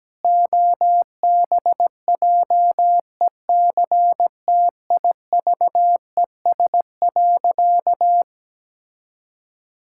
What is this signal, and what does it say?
Morse code, heard as a keyed tone, about 17 wpm, 700 Hz. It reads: OBJECTIVES.
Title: Pangram Challenge v2 - 17wpm